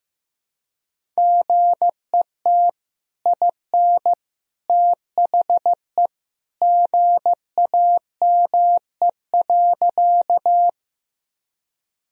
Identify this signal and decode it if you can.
Morse code, heard as a keyed tone, about 16 wpm, 700 Hz. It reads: GET IN THE GAME.
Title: Sentences from Top 1000 Words - Review Part 3 - 15wpm